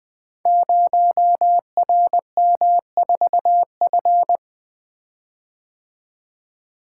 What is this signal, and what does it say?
Morse code, heard as a keyed tone, about 20 wpm, 700 Hz. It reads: 0RM4F
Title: Code Groups - 5 Letter:Numbers - 20wpm